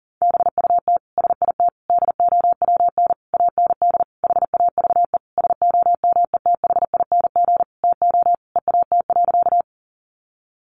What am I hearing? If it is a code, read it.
Morse code, heard as a keyed tone, about 40 wpm, 700 Hz. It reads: BUT SIT DOWN AND HAVE SOMETHING TO EAT.
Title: Sentences from Top 600 Words 40wpm